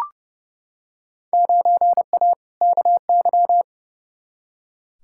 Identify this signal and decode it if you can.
Morse code, heard as a keyed tone, about 30 wpm, 700 Hz. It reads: 9A KY
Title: Field Day Exchanges Encore - 30wpm